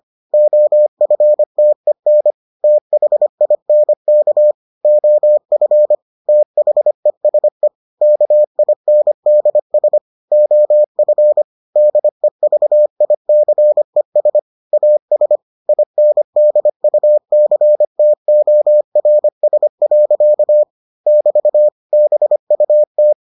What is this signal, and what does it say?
Morse code, heard as a keyed tone, about 25 wpm, 600 Hz. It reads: OFTEN THINK OF THESE KINDS OF DEVICES AS INDUCTORS. <BT> BUT